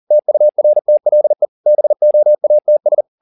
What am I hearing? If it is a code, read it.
Morse code, heard as a keyed tone, about 40 wpm, 600 Hz. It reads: TURTLE BOATS